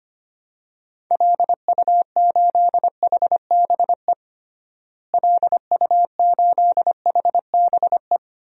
Morse code, heard as a keyed tone, about 25 wpm, 700 Hz. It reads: LU8HBE LU8HBE
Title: DXCC-100 Call Signs - 6 Characters - Repeated 3x - 25wpm